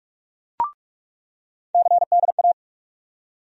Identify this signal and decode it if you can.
Morse code, heard as a keyed tone, about 48 wpm, 700 Hz. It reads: CDA
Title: USA Call Sign Postfixes - 3 Letters 45wpm